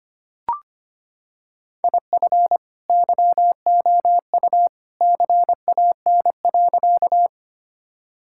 Morse code, heard as a keyed tone, about 25 wpm, 700 Hz. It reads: IF YOU CAN.
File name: practice / 25wpm / Sentences with 2-Word Phrases - 25wpm